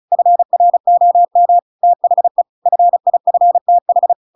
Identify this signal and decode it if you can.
Morse code, heard as a keyed tone, about 36 wpm, 700 Hz. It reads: FROM THE FIFTH
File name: Pinocchio / Pinocchio ch7 35 35 ik1hge